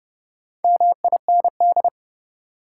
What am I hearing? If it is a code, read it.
Morse code, heard as a keyed tone, about 30 wpm, 700 Hz. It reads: MIND
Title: Sets of 5 Words - New Word Drill - 30wpm